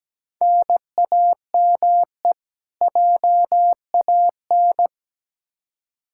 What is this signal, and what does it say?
Morse code, heard as a keyed tone, about 17 wpm, 700 Hz. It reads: NAME JAN
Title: Practice QSO - Exchange 1 - Beginning - 17wpm